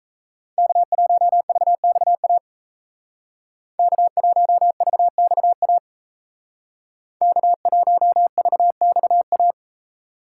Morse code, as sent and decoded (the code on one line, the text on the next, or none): -.- .---- ...- -..- .- / -.- .---- ...- -..- .- / -.- .---- ...- -..- .-
K1VXA K1VXA K1VXA